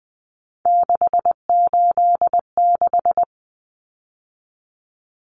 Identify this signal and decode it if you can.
Morse code, heard as a keyed tone, about 20 wpm, 700 Hz. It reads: 686
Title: Common Cut Numbers - 3-Digit 20wpm